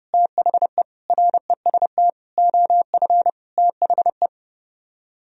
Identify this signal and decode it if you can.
Morse code, heard as a keyed tone, about 30 wpm, 700 Hz. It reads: THE REST OF THE